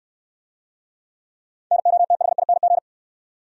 Morse code, heard as a keyed tone, about 69 wpm, 700 Hz. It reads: NOTHING